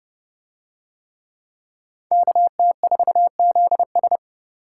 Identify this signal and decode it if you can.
Morse code, heard as a keyed tone, about 29 wpm, 700 Hz. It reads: KT4ZS